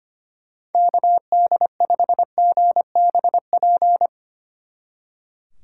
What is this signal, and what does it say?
Morse code, heard as a keyed tone, about 31 wpm, 700 Hz. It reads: KD5GBP
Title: Speed Racing - USA 2x3 Call Signs 25wpm